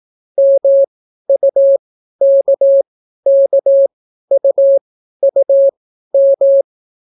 Morse code, heard as a keyed tone, about 18 wpm, spaced as for 12 wpm, 550 Hz. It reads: MUKKUUM